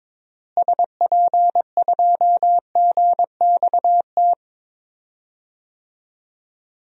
Morse code, heard as a keyed tone, about 22 wpm, 700 Hz. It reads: SP2GXT
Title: DXCC-100 Call Signs - 6 Characters - Repeated 2x - 22wpm